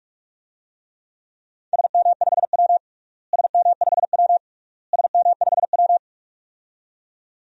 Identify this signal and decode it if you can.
Morse code, heard as a keyed tone, about 44 wpm, 700 Hz. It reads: SM5W SM5W SM5W
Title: DXCC-100 Call Signs - 4 Characters - Repeated 3x - 45wpm